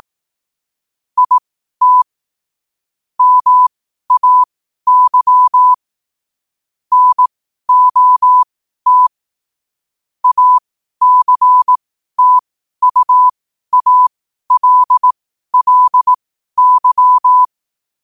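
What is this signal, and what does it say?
Morse code, heard as a keyed tone, about 18 wpm, spaced as for 12 wpm, 1000 Hz. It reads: IT MAY NOT ACTUALLY
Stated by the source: Morse code quotes at 12 words per minute.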